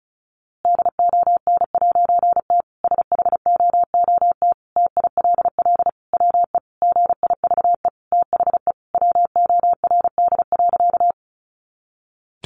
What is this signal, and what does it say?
Morse code, heard as a keyed tone, about 34 wpm, 700 Hz. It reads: DON'T SHOOT TILL WE GIVE THE WORD.
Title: A Sound Of Thunder 35wpm